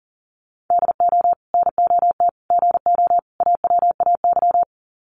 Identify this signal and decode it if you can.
Morse code, heard as a keyed tone, about 40 wpm, 700 Hz. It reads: DO NOT GO AWAY
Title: Sets of 4 Words 40wpm